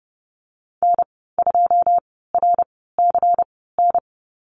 Morse code, heard as a keyed tone, about 30 wpm, spaced as for 17 wpm, 700 Hz. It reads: N2RCN
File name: Lesson 19g - Letter w - Call Signs 30wpm